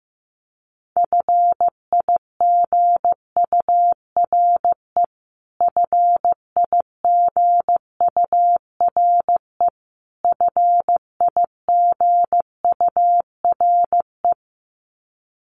Morse code, heard as a keyed tone, about 14 wpm, 700 Hz. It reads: FIGURE FIGURE FIGURE
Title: Top 500 Words Repeated 15wpm